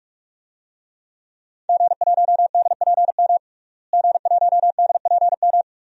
Morse code, heard as a keyed tone, about 46 wpm, 700 Hz. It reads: G1DPM G1DPM
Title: DXCC-100 Call Signs - 5 Characters - Repeated 3x - 45wpm